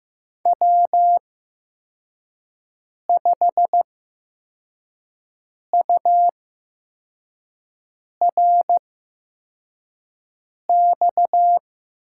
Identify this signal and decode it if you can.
Morse code, heard as a keyed tone, about 15 wpm, spaced as for 4 wpm, 700 Hz. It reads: W5URX
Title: USA Call Signs - 5 Characters - 8x Character Spacing - 15wpm